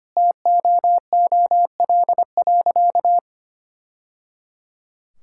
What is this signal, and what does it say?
Morse code, heard as a keyed tone, about 25 wpm, 700 Hz. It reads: TOOL.
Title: Sentences from Top 1000 Words - Review Part 6 - 25wpm